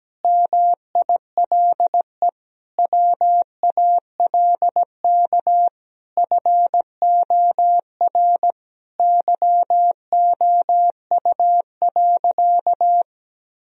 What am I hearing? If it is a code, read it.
Morse code, heard as a keyed tone, about 17 wpm, 700 Hz. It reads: MILE WALK FOR YOU.
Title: Sentences from Top 1000 Words - Review Part 1 - 17wpm